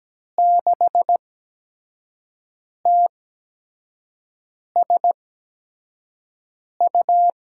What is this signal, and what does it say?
Morse code, heard as a keyed tone, about 17 wpm, spaced as for 5 wpm, 700 Hz. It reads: 6TSU
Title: USA Call Signs - 8x Character Spacing - 17wpm